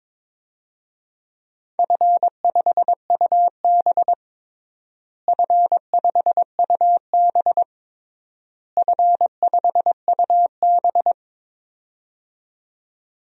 Morse code, heard as a keyed tone, about 22 wpm, 700 Hz. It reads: F5UB F5UB F5UB
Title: DXCC-100 Call Signs - 4 Characters - Repeated 3x - 22wpm